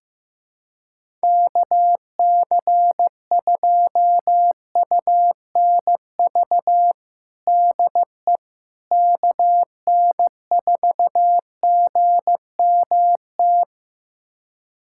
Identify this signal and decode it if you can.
Morse code, heard as a keyed tone, about 15 wpm, 700 Hz. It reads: KC2UNV DE KN4GMT
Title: USA Callsign DE Callsign - 6 Characters - 15wpm